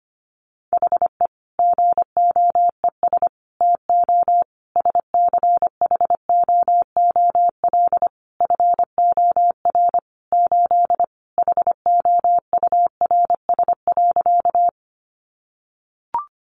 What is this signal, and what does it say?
Morse code, heard as a keyed tone, about 25 wpm, 700 Hz. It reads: HE GOES TO SCHOOL FOR 8 HOURS.
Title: Additional Easy Phrases 25wpm